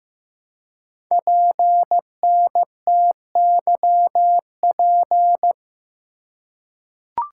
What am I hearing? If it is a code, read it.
Morse code, heard as a keyed tone, about 15 wpm, 700 Hz. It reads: PNTYP
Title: Code Groups - 5 Letter:Numbers - 15wpm